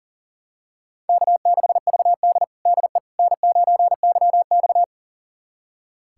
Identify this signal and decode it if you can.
Morse code, heard as a keyed tone, about 40 wpm, 700 Hz. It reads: K6VD DE N9YX
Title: USA Callsign DE Callsign - 4 Characters - 40wpm